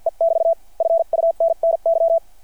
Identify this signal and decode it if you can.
Morse code, heard as a keyed tone, about 51 wpm, spaced as for 44 wpm, 660 Hz. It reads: E <BT> FUNNY
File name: archived / CRSnet